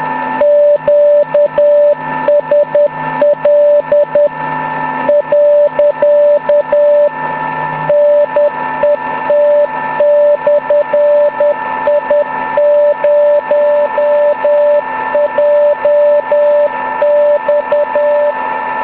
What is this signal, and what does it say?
Morse code, heard as a keyed tone, about 10 wpm, 580 Hz, with noise. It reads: QSL.NET/I0JX